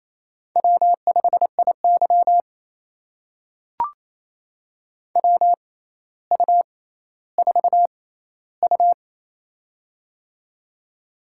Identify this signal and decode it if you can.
Morse code, heard as a keyed tone, about 27 wpm, spaced as for 14 wpm, 700 Hz. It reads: W5IY WU4U
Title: USA Call Signs - 4 Characters - 6x Character Spacing - 28wpm